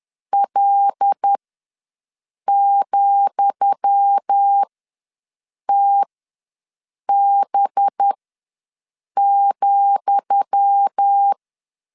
Morse code, heard as a keyed tone, about 10 wpm, spaced as for 6 wpm, 800 Hz. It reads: L,TB,
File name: LEZIONI / 10 SIGNS / 10D COMA ws 10wpm 4314